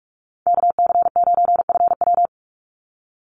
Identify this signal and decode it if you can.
Morse code, heard as a keyed tone, about 44 wpm, 700 Hz. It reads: KC9FW
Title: USA Call Signs 45wpm